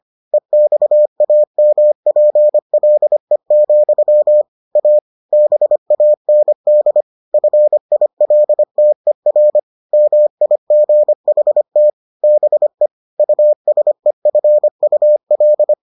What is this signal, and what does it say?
Morse code, heard as a keyed tone, about 25 wpm, 600 Hz. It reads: EXAMPLE, A BAND FILTER MIGHT BE USEFUL